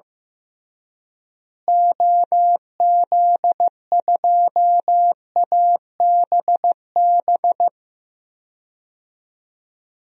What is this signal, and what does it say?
Morse code, heard as a keyed tone, about 16 wpm, 700 Hz. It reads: OZ2ABB
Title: DXCC-100 Call Signs - 6 Characters - Repeated 3x - 15wpm